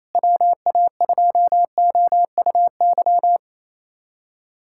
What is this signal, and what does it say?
Morse code, heard as a keyed tone, about 28 wpm, 700 Hz. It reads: WA2OUY
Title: USA Callsign DE Callsign 28wpm